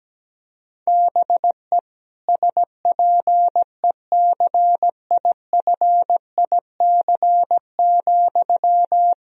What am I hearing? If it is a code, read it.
Morse code, heard as a keyed tone, about 17 wpm, 700 Hz. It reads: BE SPECIFIC,
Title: Sentences from Top 1000 Words - Review Part 3 - 17wpm